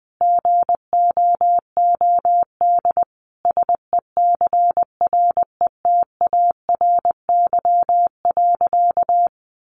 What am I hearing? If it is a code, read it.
Morse code, heard as a keyed tone, about 20 wpm, 700 Hz. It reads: GOOD SECRETARY.
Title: Sentences from Top 1300 Words 20wpm